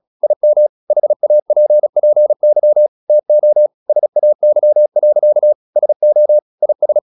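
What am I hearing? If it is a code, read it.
Morse code, heard as a keyed tone, about 36 wpm, 600 Hz. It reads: IM HAPPY TO SAY. SO IS
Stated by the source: Morse practice at 36 wpm, text of Ken Follett's novel Never, from CW med Gnister.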